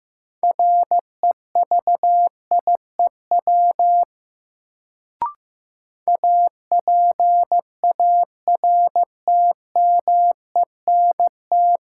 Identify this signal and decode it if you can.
Morse code, heard as a keyed tone, about 15 wpm, 700 Hz. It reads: REVIEW APARTMENT
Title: Sentences from Top 900 Words - New Word Drill - 15wpm